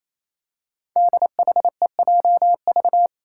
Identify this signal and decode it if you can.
Morse code, heard as a keyed tone, about 27 wpm, 700 Hz. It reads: DHEJV
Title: Five Challenging Letter-Number Combinations 28wpm